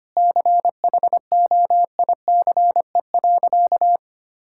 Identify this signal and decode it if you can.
Morse code, heard as a keyed tone, about 25 wpm, 700 Hz. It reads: CHOICE.
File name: practice / 25wpm / Sentences with 3-Word Phrases - 25wpm